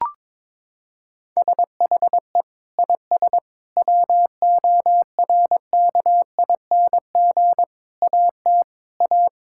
Morse code, heard as a keyed tone, about 22 wpm, 700 Hz. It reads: SHE IS WORKING AT A
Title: Sentences from Top 1000 Words - Review Part 5 - 22wpm